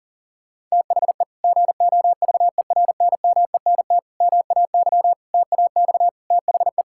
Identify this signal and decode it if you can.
Morse code, heard as a keyed tone, about 40 wpm, 700 Hz. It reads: THE GOVERNMENT MAY TAX THE
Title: Sentences from Top 1000 Words - Review Part 5 - 40wpm